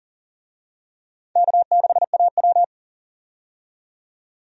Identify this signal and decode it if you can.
Morse code, heard as a keyed tone, about 40 wpm, 700 Hz. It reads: K6AW
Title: USA Call Signs - 4 Characters - 40wpm